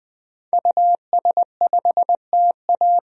Morse code, heard as a keyed tone, about 20 wpm, 700 Hz. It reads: US5TA
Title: DXCC-100 Call Signs - 5 Characters - 20wpm